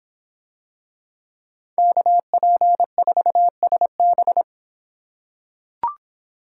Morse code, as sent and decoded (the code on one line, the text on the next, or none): -.- .--. ....- ... -...
KP4SB